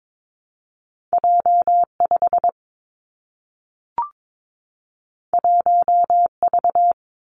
Morse code, heard as a keyed tone, about 22 wpm, 700 Hz. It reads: J 5 1 V
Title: Two Challenging Letter-Number Combinations 22wpm